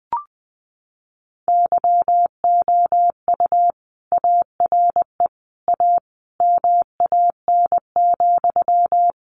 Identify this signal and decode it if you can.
Morse code, heard as a keyed tone, about 20 wpm, 700 Hz. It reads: YOU ARE A MAN,
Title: Sentences from Top 200 Words 20wpm